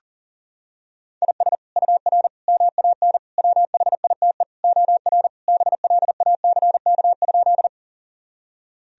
Morse code, heard as a keyed tone, about 40 wpm, 700 Hz. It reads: IS UR MAN WHITE OR BLACK?
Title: Sentences from Top 400 Words and Abbreviations 40wpm